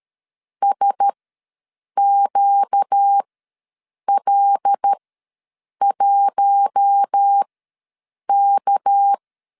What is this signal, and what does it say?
Morse code, heard as a keyed tone, about 13 wpm, spaced as for 7 wpm, 800 Hz. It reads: SQL1K
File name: LEZIONI / 6 PKX0 / 6 PK ws 12wpm 3116